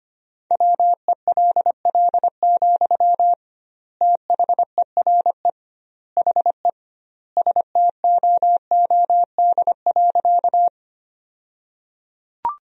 Morse code, heard as a keyed tone, about 25 wpm, 700 Hz. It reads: WELL, THERE HE STOOD.
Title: 2x Word-Spacing - Sentences from Top 500 Words - 25wpm